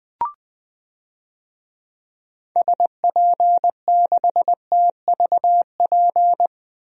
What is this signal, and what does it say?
Morse code, heard as a keyed tone, about 20 wpm, 700 Hz. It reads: SP6TVP
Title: DXCC-100 Call Signs - 6 Characters - Repeated 2x - 20wpm